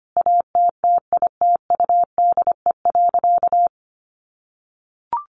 Morse code, heard as a keyed tone, about 25 wpm, 700 Hz. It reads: ATTITUDE.
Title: Sentences from Top 1300 Words 25wpm